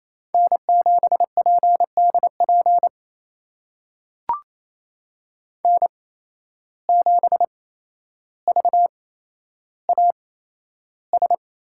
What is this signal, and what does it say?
Morse code, heard as a keyed tone, about 28 wpm, spaced as for 12 wpm, 700 Hz. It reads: N7PDP N7VAS
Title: USA Call Signs - 5 Characters - 8x Character Spacing - 28wpm